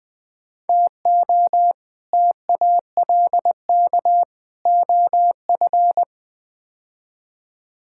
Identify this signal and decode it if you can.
Morse code, heard as a keyed tone, about 21 wpm, 700 Hz. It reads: TO TALK OF
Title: Build A Phrase - 5-Word Phrases - 20wpm